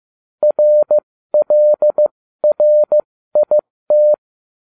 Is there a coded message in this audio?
Yes: Morse code from 0.4 to 4.1 s, heard as a keyed tone, about 15 wpm, spaced as for 13 wpm, 600 Hz. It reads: RLRIT